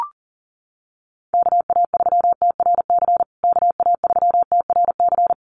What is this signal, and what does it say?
Morse code, heard as a keyed tone, about 40 wpm, 700 Hz. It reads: KA3TRC KA3TRC
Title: USA Call Signs Repeated 3x 40wpm